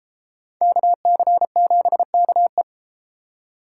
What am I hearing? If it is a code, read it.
Morse code, heard as a keyed tone, about 33 wpm, 700 Hz. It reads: KC7KE